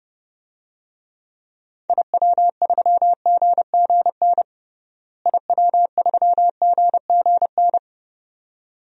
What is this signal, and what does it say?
Morse code, heard as a keyed tone, about 30 wpm, 700 Hz. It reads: IW3GGN IW3GGN
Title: DXCC-100 Call Signs - 6 Characters - Repeated 2x - 30wpm